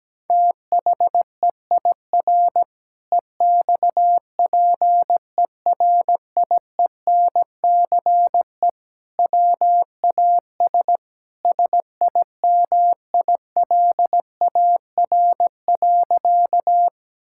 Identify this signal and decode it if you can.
Morse code, heard as a keyed tone, about 17 wpm, 700 Hz. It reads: THEIR EXPERIENCE WAS SIMILAR.
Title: Sentences from Top 1000 Words - Review Part 5 - 17wpm